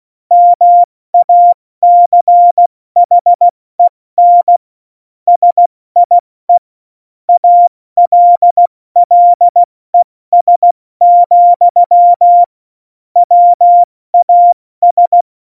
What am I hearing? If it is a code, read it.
Morse code, heard as a keyed tone, about 16 wpm, 700 Hz. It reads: MACHEN SIE ALLES, WAS
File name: Morsekurs / Witze / Witz 05. Februar 2019, 70 F80